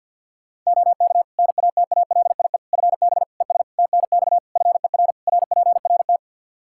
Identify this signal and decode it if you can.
Morse code, heard as a keyed tone, about 52 wpm, 700 Hz. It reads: OK NATALIE FB ES TNX FER RPRT
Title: Practice QSO - Exchange 2 - Info - 50wpm